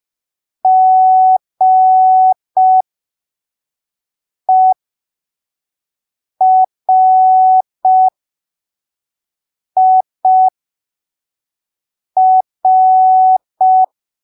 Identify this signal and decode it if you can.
Morse code, heard as a keyed tone, about 5 wpm, spaced as for 3 wpm, 750 Hz. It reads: GERIR
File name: arquivos / texto / texto 06 05ppm